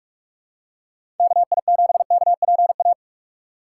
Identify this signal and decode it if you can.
Morse code, heard as a keyed tone, about 45 wpm, 700 Hz. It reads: KI7KPA
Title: USA Call Signs - 6x Character Spacing - 45wpm